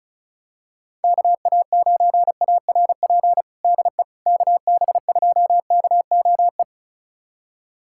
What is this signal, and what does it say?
Morse code, heard as a keyed tone, about 34 wpm, 700 Hz. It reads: KA9ARP DE KB2KOE
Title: USA Callsign DE Callsign - 6 Characters - 35wpm